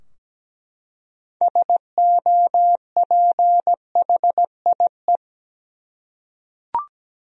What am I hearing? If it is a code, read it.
Morse code, heard as a keyed tone, about 17 wpm, 700 Hz. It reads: SOPHIE
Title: USA Common Names - 17wpm